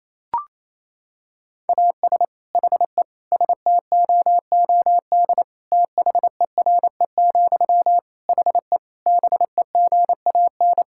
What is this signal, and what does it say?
Morse code, heard as a keyed tone, about 28 wpm, 700 Hz. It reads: AS HE STOOD THERE, HE BEGAN
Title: Sentences from Top 1600 Words 28wpm